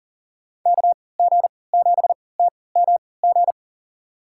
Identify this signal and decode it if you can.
Morse code, heard as a keyed tone, about 39 wpm, spaced as for 22 wpm, 700 Hz. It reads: KG7TMG
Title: USA Call Signs - 3x Character Spacing - 40wpm